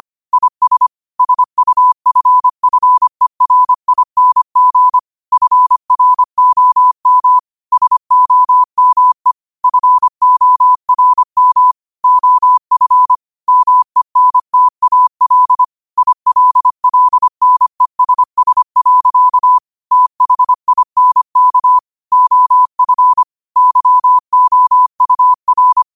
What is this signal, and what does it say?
Morse code, heard as a keyed tone, about 25 wpm, 1000 Hz. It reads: IS SUFFERING FROM SOME FORM OF MENTAL ILLNESS. THINK OF YOUR
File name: qotdfiles / 25wpm